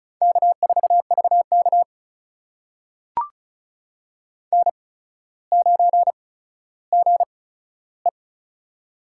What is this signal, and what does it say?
Morse code, heard as a keyed tone, about 36 wpm, spaced as for 15 wpm, 700 Hz. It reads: K4VK N9GE
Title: USA Call Signs - 4 Characters - 8x Character Spacing - 35wpm